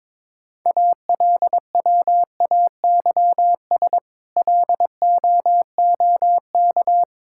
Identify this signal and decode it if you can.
Morse code, heard as a keyed tone, about 22 wpm, 700 Hz. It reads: ALWAYS LOOK
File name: Difficult Sentences with Hints - 22wpm